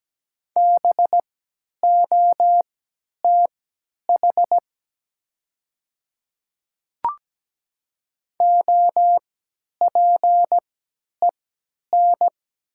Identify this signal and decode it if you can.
Morse code, heard as a keyed tone, about 17 wpm, spaced as for 10 wpm, 700 Hz. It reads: BOTH OPEN
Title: 3x Character Spacing - Top 500 Words - 17wpm